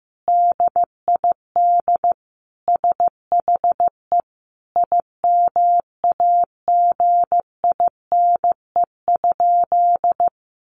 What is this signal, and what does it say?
Morse code, heard as a keyed tone, about 15 wpm, 700 Hz. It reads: DID SHE IMAGINE?
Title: Sentences from Top 1400 Words 15wpm